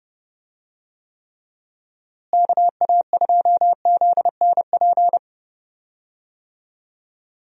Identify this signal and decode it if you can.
Morse code, heard as a keyed tone, about 30 wpm, 700 Hz. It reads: KA2ZNP
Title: USA Call Signs - 6 Characters - Repeated 2x - 30wpm